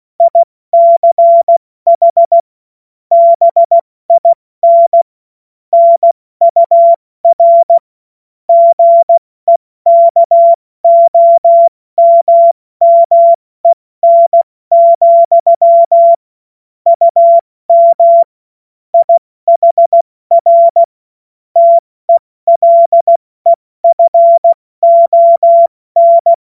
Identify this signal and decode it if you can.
Morse code, heard as a keyed tone, about 16 wpm, 680 Hz. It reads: ICH BIN NUR GEKOMMEN, UM IHR TELEFON
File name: Morsekurs / Witze / Witz 10. Mai 2022 70 F80